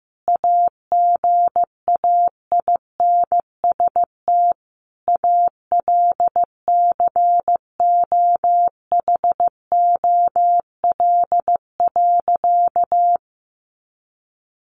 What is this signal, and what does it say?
Morse code, heard as a keyed tone, about 15 wpm, 700 Hz. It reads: AGAINST ALCOHOL.
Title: Sentences from Top 1400 Words 15wpm